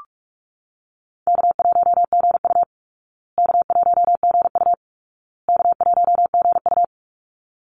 Morse code, heard as a keyed tone, about 44 wpm, 700 Hz. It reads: K1GU K1GU K1GU
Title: Easy Call Signs Repeated 3x - 45wpm